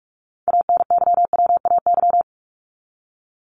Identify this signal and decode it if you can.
Morse code, heard as a keyed tone, about 44 wpm, 700 Hz. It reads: ANYWAY
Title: Top 1501-1750 Words 45wpm